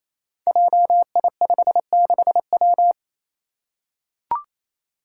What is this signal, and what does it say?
Morse code, heard as a keyed tone, about 28 wpm, 700 Hz. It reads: JI56W
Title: Five Challenging Letter-Number Combinations 28wpm